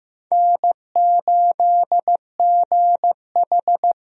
Morse code, heard as a keyed tone, about 15 wpm, 700 Hz. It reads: N8GH
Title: USA Call Signs - 4 Characters - Repeated 3x - 15wpm